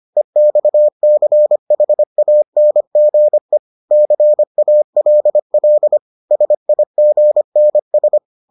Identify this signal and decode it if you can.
Morse code, heard as a keyed tone, about 25 wpm, 600 Hz. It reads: EXCHANGE CALL SIGNS